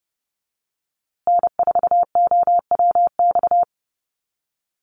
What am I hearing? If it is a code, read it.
Morse code, heard as a keyed tone, about 30 wpm, 700 Hz. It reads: N4OWX
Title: USA Call Signs Repeated 3x 30wpm